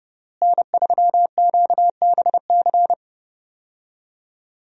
Morse code, heard as a keyed tone, about 29 wpm, 700 Hz. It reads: N3QBC
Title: USA Call Signs - 5 Characters - Repeated 2x - 30wpm